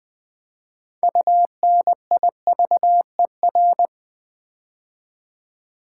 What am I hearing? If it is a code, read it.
Morse code, heard as a keyed tone, about 20 wpm, 700 Hz. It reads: UNIVER
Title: Build A Word - 3 Letters At A Time - 9-Letter Words - 20wpm